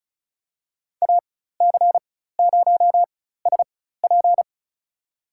Morse code, heard as a keyed tone, about 34 wpm, spaced as for 16 wpm, 700 Hz. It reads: AC0SP
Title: USA Call Signs - 5 Characters - 4x Character Spacing - 35wpm